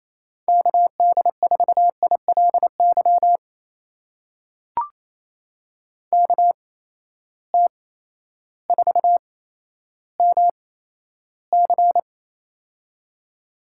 Morse code, heard as a keyed tone, about 28 wpm, spaced as for 13 wpm, 700 Hz. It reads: KD4ILY KT4MC
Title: USA Call Signs - 8x Character Spacing - 28wpm